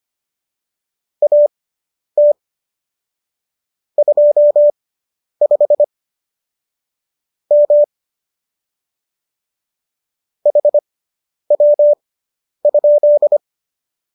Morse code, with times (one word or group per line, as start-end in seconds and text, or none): 1.2-2.3 AT
4.0-5.8 25
7.5-7.8 M
10.5-13.4 HW?